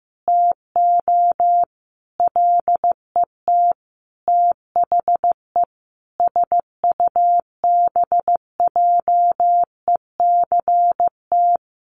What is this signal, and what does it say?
Morse code, heard as a keyed tone, about 15 wpm, 700 Hz. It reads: TO LET THE SUBJECT
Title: Sentences from Top 1400 Words 15wpm